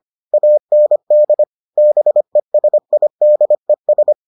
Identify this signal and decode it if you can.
Morse code, heard as a keyed tone, about 25 wpm, 600 Hz. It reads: AND BESIDES